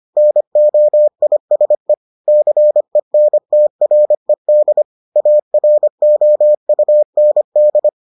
Morse code, heard as a keyed tone, about 25 wpm, 600 Hz. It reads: NOISE CENTRED AROUND